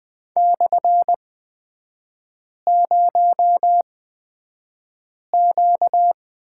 Morse code, heard as a keyed tone, about 20 wpm, spaced as for 5 wpm, 690 Hz. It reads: /0Q
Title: 236 Lesson 38e - Letter q - Three Characters 20wpm